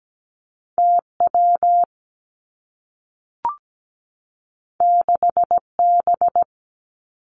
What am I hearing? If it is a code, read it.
Morse code, heard as a keyed tone, about 17 wpm, 700 Hz. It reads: T W 6 B
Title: Two Challenging Letter-Number Combinations 17wpm